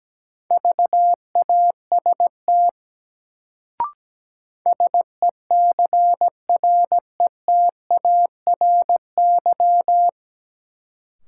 Morse code, heard as a keyed tone, about 17 wpm, 700 Hz. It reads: VAST SECRETARY
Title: Sentences from Top 1300 Words - New Word Drill - 17wpm